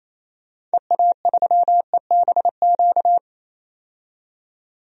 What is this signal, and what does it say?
Morse code, heard as a keyed tone, about 28 wpm, 700 Hz. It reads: EA3EBQ
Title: DXCC-100 Call Signs - 6 Characters - 28wpm